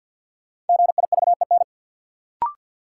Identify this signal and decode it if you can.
Morse code, heard as a keyed tone, about 52 wpm, 700 Hz. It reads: GIVEN
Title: Sentences from Top 1400 Words - New Word Drill - 50wpm